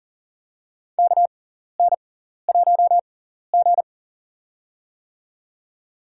Morse code, heard as a keyed tone, about 40 wpm, spaced as for 14 wpm, 700 Hz. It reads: KN1G
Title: USA Call Signs - 6x Character Spacing - 40wpm